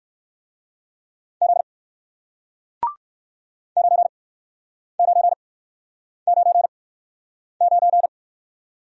Morse code, heard as a keyed tone, about 60 wpm, spaced as for 9 wpm, 700 Hz. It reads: Z 9999